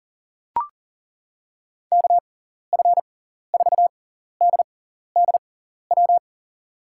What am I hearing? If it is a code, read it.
Morse code, heard as a keyed tone, about 40 wpm, spaced as for 14 wpm, 700 Hz. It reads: KF4DDW
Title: USA Call Signs - 6 Characters - 6x Character Spacing - 40wpm